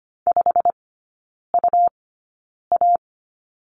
Morse code, heard as a keyed tone, about 25 wpm, spaced as for 9 wpm, 700 Hz. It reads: 5UA
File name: Lesson 17e - Letter u - Three Characters 25wpm